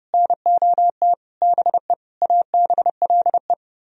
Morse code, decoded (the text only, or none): NOT BE ABLE